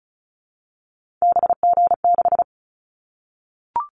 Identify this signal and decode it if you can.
Morse code, heard as a keyed tone, about 36 wpm, 700 Hz. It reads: BG6
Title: Callsign Prefixes for Top 100 DXCC - 35wpm